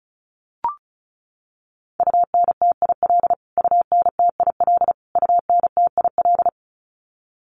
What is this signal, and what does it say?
Morse code, heard as a keyed tone, about 34 wpm, 700 Hz. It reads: UNTIL UNTIL UNTIL
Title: Top 500 Words Repeated 35wpm